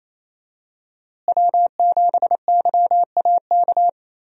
Morse code, heard as a keyed tone, about 28 wpm, 700 Hz. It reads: W7YAK
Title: USA Call Signs - 5 Characters - Repeated 3x - 28wpm